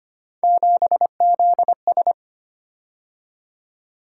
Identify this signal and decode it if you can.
Morse code, heard as a keyed tone, about 25 wpm, 700 Hz. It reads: 7ZS
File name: Three Letter-Number Combinations - 25wpm